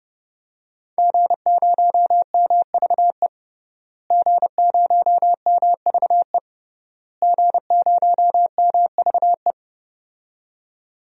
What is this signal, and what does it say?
Morse code, heard as a keyed tone, about 31 wpm, 700 Hz. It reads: G0MVE G0MVE G0MVE
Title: DXCC-100 Call Signs - 5 Characters - Repeated 3x - 30wpm